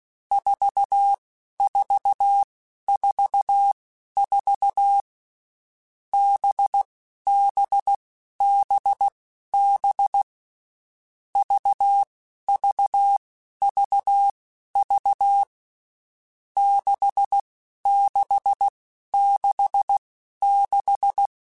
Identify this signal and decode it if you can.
Morse code, heard as a keyed tone, about 16 wpm, spaced as for 11 wpm, 800 Hz. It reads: 4444 BBBB VVVV 6666
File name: Groups@10 / Groups@10 mp3s / VB46@10